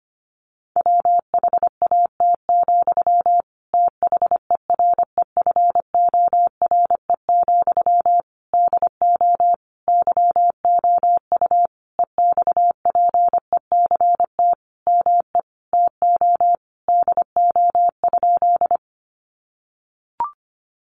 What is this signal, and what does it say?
Morse code, heard as a keyed tone, about 25 wpm, 700 Hz. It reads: WHAT, THEREFORE, DO YOU EXPECT ME TO DO?
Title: Sentences from Top 1100 Words 25wpm